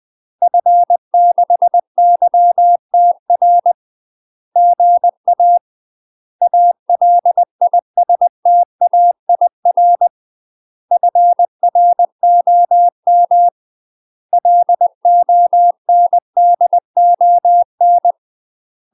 Morse code, heard as a keyed tone, about 20 wpm, 700 Hz. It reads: F6YTR GA ALISTAIR FROM LONDON
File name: UK MASTER QSOs TRAINING / 20WPM / UK MASTER QSOs 0004